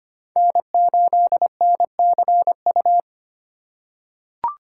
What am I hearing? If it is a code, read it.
Morse code, heard as a keyed tone, about 25 wpm, 700 Hz. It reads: N8NCU
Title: USA Call Signs - 5 Characters - 4x Character Spacing - 25wpm